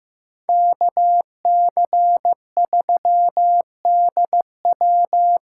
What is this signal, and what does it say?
Morse code, heard as a keyed tone, about 15 wpm, 700 Hz. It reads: KC3DW
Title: USA Call Signs - 6 Characters - 6x Character Spacing - 15wpm